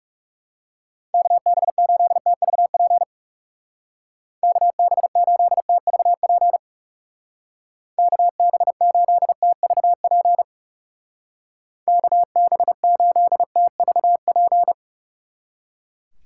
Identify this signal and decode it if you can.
Morse code, heard as a keyed tone, about 38 wpm, 700 Hz. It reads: KB8TVP KB8TVP KB8TVP KB8TVP